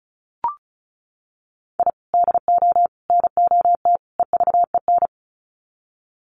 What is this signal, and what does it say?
Morse code, heard as a keyed tone, about 34 wpm, 700 Hz. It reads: I DO NOT EVEN
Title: Sets of 4 Words 35wpm